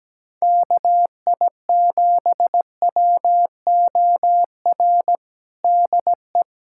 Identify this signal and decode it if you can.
Morse code, heard as a keyed tone, about 17 wpm, 700 Hz. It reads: KI7WOR DE
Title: USA Callsign DE Callsign - 6 Characters - 17wpm